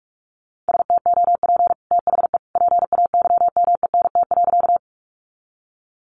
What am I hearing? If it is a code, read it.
Morse code, heard as a keyed tone, about 44 wpm, 700 Hz. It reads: STOP THE PAYMENT.
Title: Sentences from Top 1200 Words 45wpm